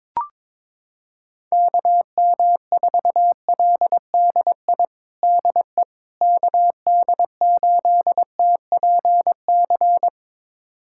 Morse code, heard as a keyed tone, about 22 wpm, 700 Hz. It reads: KM4LDI DE KD8TPC
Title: USA Callsign DE Callsign - 6 Characters - 22wpm